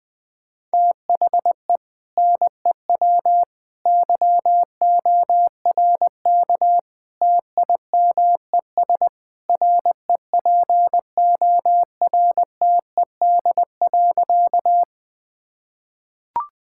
Morse code, heard as a keyed tone, about 21 wpm, 700 Hz. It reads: THE NEW YORK TIMES REPORTED.
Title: Sentences with 3-Word Phrases - Spoken Beginning - 20wpm